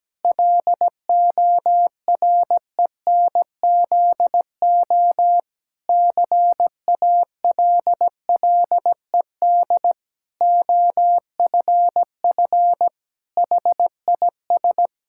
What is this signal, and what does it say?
Morse code, heard as a keyed tone, about 17 wpm, 700 Hz. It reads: LORENZO CALLED OFF HIS
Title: Pangram Challenge v2 - 17wpm